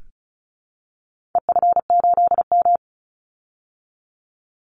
Morse code, heard as a keyed tone, about 34 wpm, 700 Hz. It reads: EF8M
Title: Call Signs Repeated 2x - 35wpm